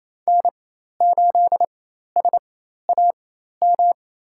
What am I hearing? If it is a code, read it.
Morse code, heard as a keyed tone, about 28 wpm, spaced as for 13 wpm, 700 Hz. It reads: N8SAM
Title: USA Call Signs - 5 Characters - 4x Character Spacing - 28wpm